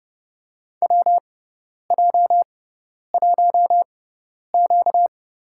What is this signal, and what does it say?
Morse code, heard as a keyed tone, about 31 wpm, spaced as for 10 wpm, 700 Hz. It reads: WJ1Q
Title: USA Call Signs - 4 Characters - 6x Character Spacing - 30wpm